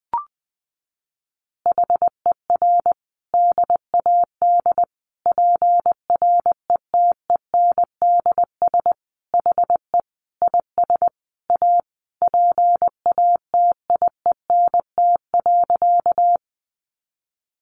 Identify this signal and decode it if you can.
Morse code, heard as a keyed tone, about 20 wpm, 700 Hz. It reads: HER DAD PRETENDS HE IS A PATIENT.
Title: Additional Easy Phrases 20wpm